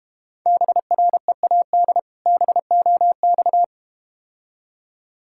Morse code, heard as a keyed tone, about 32 wpm, 700 Hz. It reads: BREAD BOX